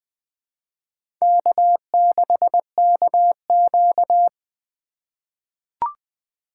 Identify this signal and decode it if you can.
Morse code, heard as a keyed tone, about 20 wpm, 700 Hz. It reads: K6KQ
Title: USA Call Signs - 4 Characters - 3x Character Spacing - 20wpm